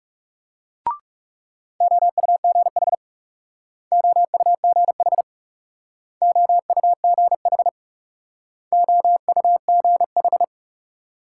Morse code, heard as a keyed tone, about 38 wpm, 700 Hz. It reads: OUGH OUGH OUGH OUGH